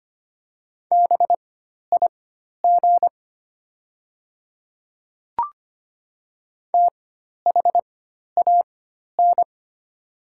Morse code, heard as a keyed tone, about 25 wpm, spaced as for 12 wpm, 700 Hz. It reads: BIG THAN
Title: 4x Character Spacing - Top 200 Words - 25wpm